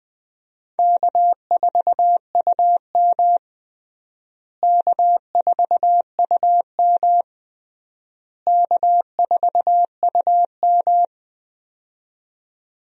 Morse code, heard as a keyed tone, about 21 wpm, 700 Hz. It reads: K4UM K4UM K4UM
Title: USA Call Signs - 4 Characters - Repeated 3x - 20wpm